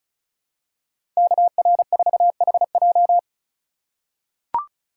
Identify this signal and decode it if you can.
Morse code, heard as a keyed tone, about 36 wpm, 700 Hz. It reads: KR4HJ
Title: USA Call Signs - 5 Characters - 2x Character Spacing - 35wpm